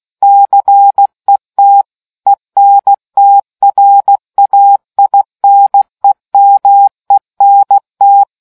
Morse code, heard as a keyed tone, about 16 wpm, 800 Hz. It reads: CET ENTRAINEMENT